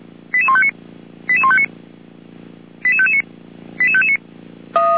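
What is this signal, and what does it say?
Five-tone selective call (ZVEI), heard as quick runs of five tones, tones of about 70 ms. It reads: 88159 88159 88559 88559
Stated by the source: ZVEI five-tone call received on SDR.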